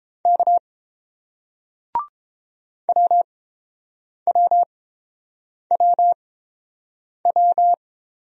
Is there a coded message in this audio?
Yes: Morse code from 0.2 to 7.7 s, heard as a keyed tone, about 30 wpm, spaced as for 7 wpm, 700 Hz. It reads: K WWWW